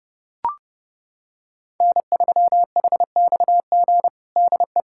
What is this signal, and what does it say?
Morse code, heard as a keyed tone, about 30 wpm, 700 Hz. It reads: N3HXG DE
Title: Practice QSO - Exchange 4 - Ending - 30wpm